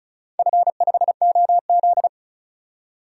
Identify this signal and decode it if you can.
Morse code, heard as a keyed tone, about 36 wpm, 700 Hz. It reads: F5OZ
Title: DXCC-100 Call Signs - 4 Characters - 35wpm